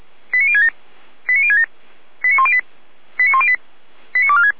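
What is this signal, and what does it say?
Five-tone selective call (ZVEI), heard as quick runs of five tones, tones of about 70 ms. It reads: 89067 89067 89108 89108 89246
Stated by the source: ZVEI five-tone call 89078 received by SDR.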